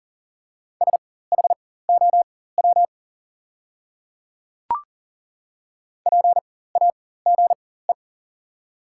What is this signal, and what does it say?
Morse code, heard as a keyed tone, about 40 wpm, spaced as for 19 wpm, 700 Hz. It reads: SHOW PAGE
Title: 4x Character Spacing - Top 500 Words - 40wpm